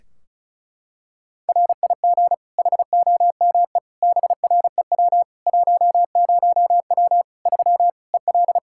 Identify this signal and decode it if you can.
Morse code, heard as a keyed tone, about 36 wpm, 700 Hz. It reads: RIG HOME BREW 10W 3 EL
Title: Practice QSO - Exchange 2 - Info - 35wpm